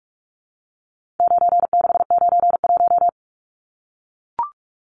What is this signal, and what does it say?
Morse code, heard as a keyed tone, about 44 wpm, 700 Hz. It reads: 9691
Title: Numbers - 4-Digit 45wpm